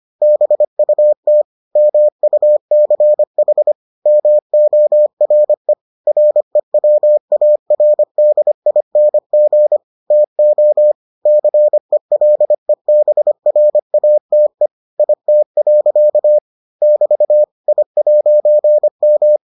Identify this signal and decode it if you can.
Morse code, heard as a keyed tone, about 25 wpm, 600 Hz. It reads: BUT MUCH MORE REWARDING TO CELEBRATE IT. <BT> I'M